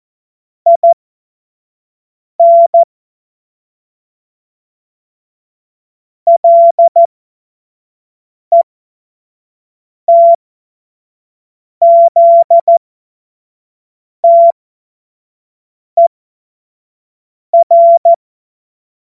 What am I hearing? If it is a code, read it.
Morse code, heard as a keyed tone, about 14 wpm, spaced as for 5 wpm, 680 Hz. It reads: IN LETZTER
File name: Morsekurs / Witze / Witz 13 Mai 2025 25 F70